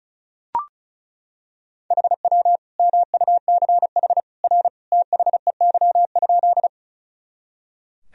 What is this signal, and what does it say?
Morse code, heard as a keyed tone, about 34 wpm, 700 Hz. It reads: HW MUCH R THEY?
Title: Sentences from Top 200 Words and Abbreviations 35wpm